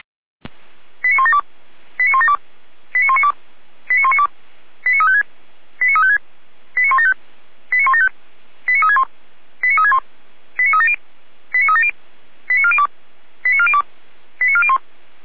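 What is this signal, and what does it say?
Five-tone selective call (ZVEI), heard as quick runs of five tones, tones of about 80 ms. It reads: 89172 89172 89192 89192 89356 89356 89175 89175 89361 89361 89370 89370 89402 89402 89401
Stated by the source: ZVEI five-tone selective call received on an SDR.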